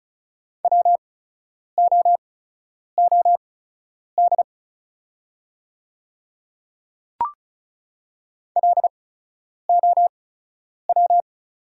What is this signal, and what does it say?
Morse code, heard as a keyed tone, about 36 wpm, spaced as for 10 wpm, 690 Hz. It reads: WOOD LOW
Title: 8x Character Spacing - Top 500 Words - 35wpm